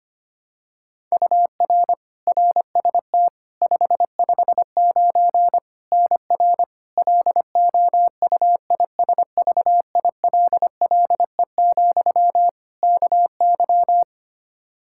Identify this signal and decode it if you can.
Morse code, heard as a keyed tone, about 25 wpm, 700 Hz. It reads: UR RST 559 NR LOUISVILLE, KY
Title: K1USN SST QSO Practice 25wpm